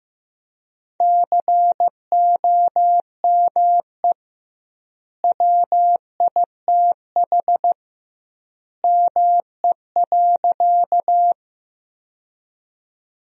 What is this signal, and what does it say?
Morse code, heard as a keyed tone, about 15 wpm, 700 Hz. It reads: COME WITH ME.
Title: 2x Word-Spacing - Sentences from Top 200 Words - 15wpm